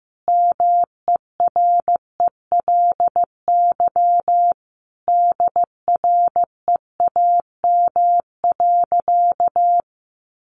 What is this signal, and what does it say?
Morse code, heard as a keyed tone, about 15 wpm, 700 Hz. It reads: MERELY DREAM.
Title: Sentences from Top 1500 Words 15wpm